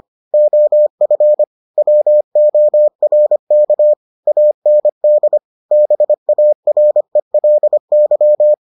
Morse code, heard as a keyed tone, about 25 wpm, 600 Hz. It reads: OF WORK AND BARELY